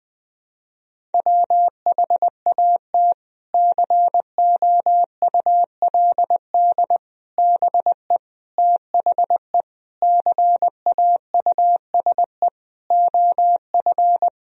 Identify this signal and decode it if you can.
Morse code, heard as a keyed tone, about 21 wpm, 700 Hz. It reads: WHAT COULD BE THE CAUSE OF
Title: Sentences from Top 1000 Words - Review Part 1 - 20wpm